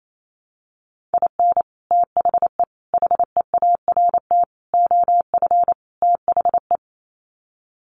Morse code, heard as a keyed tone, about 28 wpm, 700 Hz. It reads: IN THE HEART OF THE
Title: Sets of 5 Words 28wpm